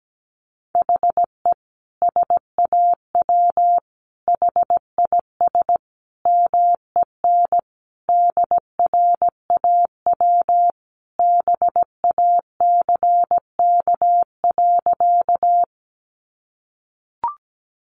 Morse code, heard as a keyed tone, about 17 wpm, 700 Hz. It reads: HE SAW HIS MEN DRAW BACK.
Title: Sentences from Top 300 Words 17wpm